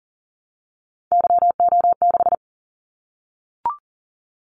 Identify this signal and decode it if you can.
Morse code, heard as a keyed tone, about 38 wpm, 700 Hz. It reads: YO6
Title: Callsign Prefixes for Top 25 DXCC - 40wpm